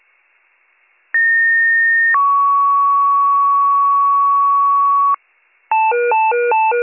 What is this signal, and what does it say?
Two-tone page, heard as a one-second tone, then a three-second tone, about 1850 then 1120 Hz. It is heard from 1.1 to 5.2 s.